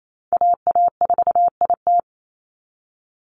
Morse code, heard as a keyed tone, about 28 wpm, 700 Hz. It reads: AA4IT
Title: NA SOTA Chaser Callsigns 28wpm